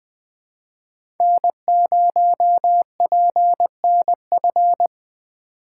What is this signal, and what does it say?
Morse code, heard as a keyed tone, about 20 wpm, 700 Hz. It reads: N0PNF
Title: USA Call Signs - 5 Characters - 4x Character Spacing - 20wpm